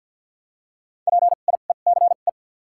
Morse code, heard as a keyed tone, about 53 wpm, 700 Hz. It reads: P I E C E
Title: 2x Character Spacing - Top 500 Words - 50wpm